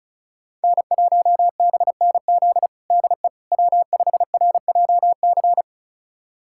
Morse code, heard as a keyed tone, about 36 wpm, 700 Hz. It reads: N1BNZ DE W5RJC
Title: USA Callsign DE Callsign - 5 Characters - 35wpm